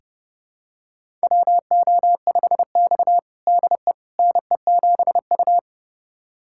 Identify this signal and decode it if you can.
Morse code, heard as a keyed tone, about 30 wpm, 700 Hz. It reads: WO5X DE NE7U
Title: USA Callsign DE Callsign - 4 Characters - 30wpm